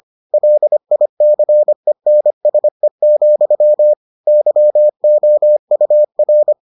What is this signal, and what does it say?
Morse code, heard as a keyed tone, about 25 wpm, 600 Hz. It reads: LICENSE, YOUR